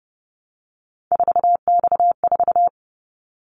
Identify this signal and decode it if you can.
Morse code, heard as a keyed tone, about 30 wpm, 700 Hz. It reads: 4X4
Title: Callsign Prefixes for Top 50 DXCC - 30wpm